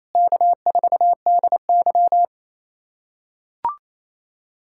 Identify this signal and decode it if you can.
Morse code, heard as a keyed tone, about 28 wpm, 700 Hz. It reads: K4DY
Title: USA Callsign DE Callsign - 4 Characters - 28wpm